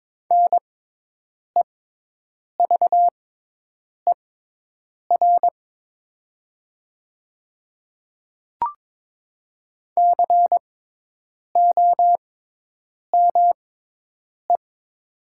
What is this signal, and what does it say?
Morse code, heard as a keyed tone, about 22 wpm, spaced as for 8 wpm, 700 Hz. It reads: NEVER COME
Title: 6x Character Spacing - Top 300 Words - 22wpm